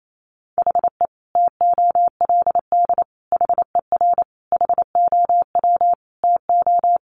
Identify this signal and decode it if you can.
Morse code, heard as a keyed tone, about 28 wpm, 700 Hz. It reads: HE TOLD HER HOW TO
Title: Sentences from Top 1500 Words 28wpm